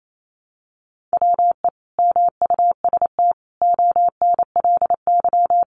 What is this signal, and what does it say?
Morse code, heard as a keyed tone, about 28 wpm, 700 Hz. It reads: WE MUST ONLY
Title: Sentences from Top 1000 Words 28wpm